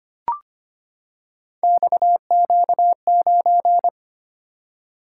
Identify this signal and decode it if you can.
Morse code, heard as a keyed tone, about 25 wpm, 700 Hz. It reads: XQ9
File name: Three Letter-Number Combinations - 25wpm